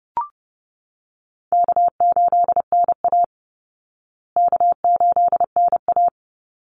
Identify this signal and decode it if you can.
Morse code, heard as a keyed tone, about 30 wpm, 700 Hz. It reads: K8NA K8NA
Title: Easy Call Signs Repeated 3x - 30wpm